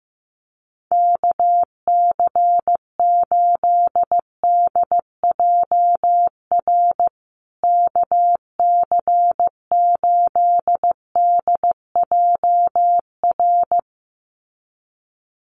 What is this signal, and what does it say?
Morse code, heard as a keyed tone, about 15 wpm, 700 Hz. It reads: KC8DJR KC8DJR
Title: USA Call Signs Repeated 2x 15wpm